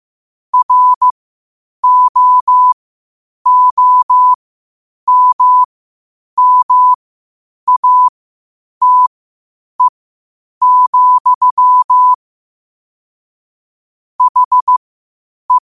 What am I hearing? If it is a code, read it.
Morse code, heard as a keyed tone, about 15 wpm, spaced as for 8 wpm, 1000 Hz. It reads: ROOMMATE, HE